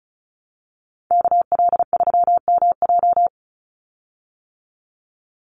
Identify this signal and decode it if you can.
Morse code, heard as a keyed tone, about 34 wpm, 700 Hz. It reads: KL3MJ
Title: USA Call Signs 35wpm